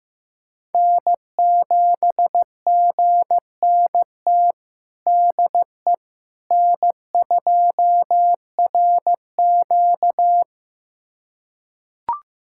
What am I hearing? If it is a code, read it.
Morse code, heard as a keyed tone, about 15 wpm, 700 Hz. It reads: N7GNT DE N2RQ
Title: USA Callsign DE Callsign 15wpm